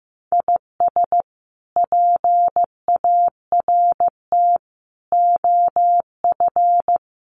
Morse code, heard as a keyed tone, about 15 wpm, 700 Hz. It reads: IS PART OF
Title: Sets of 4 Words 15wpm